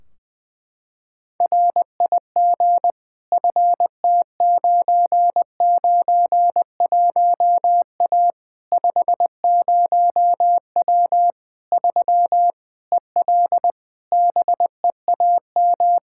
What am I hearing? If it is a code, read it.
Morse code, heard as a keyed tone, about 20 wpm, 700 Hz. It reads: RIG FT991A 50W 3 EL BEAM
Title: Practice QSO - Exchange 2 - Info - 20wpm